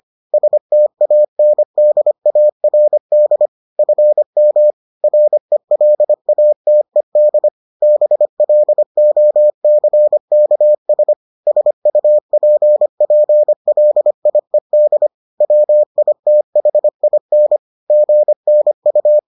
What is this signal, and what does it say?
Morse code, heard as a keyed tone, about 25 wpm, 600 Hz. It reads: STANDARD FM RELATED BLOCKS SUPPLIED WITHIN GNU